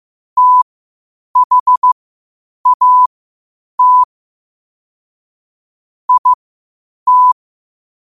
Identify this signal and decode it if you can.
Morse code, heard as a keyed tone, about 15 wpm, spaced as for 8 wpm, 1000 Hz. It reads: THAT IT
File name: qotdfiles / 8wpm